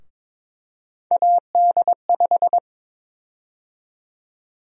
Morse code, heard as a keyed tone, about 22 wpm, 700 Hz. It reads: AD5
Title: USA Call Sign Prefixes - 22wpm